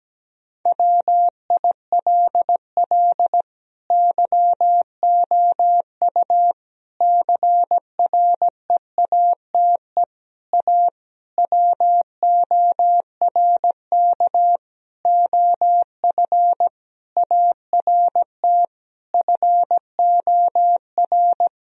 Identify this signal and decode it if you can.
Morse code, heard as a keyed tone, about 17 wpm, 700 Hz. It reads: WILL YOU CREATE A WORK OF ART FOR
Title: Sentences from Top 1000 Words - Review Part 3 - 17wpm